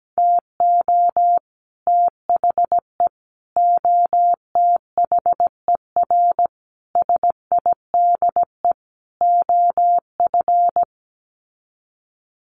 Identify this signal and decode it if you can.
Morse code, heard as a keyed tone, about 17 wpm, 700 Hz. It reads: TO THE OTHER SIDE OF
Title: Sets of 5 Words Encore 17wpm